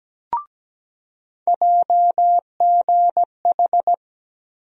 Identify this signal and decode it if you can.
Morse code, heard as a keyed tone, about 17 wpm, 700 Hz. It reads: JGH
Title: USA Call Sign Postfixes - 3 Letters 17wpm